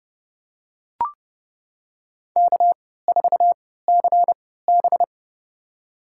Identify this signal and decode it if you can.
Morse code, heard as a keyed tone, about 30 wpm, spaced as for 17 wpm, 700 Hz. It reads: K4CB
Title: USA Call Signs - 4 Characters - 3x Character Spacing - 30wpm